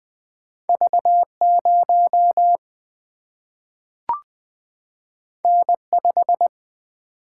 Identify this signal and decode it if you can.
Morse code, heard as a keyed tone, about 20 wpm, 700 Hz. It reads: V0 N5
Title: Single-Letter Single-Number 20wpm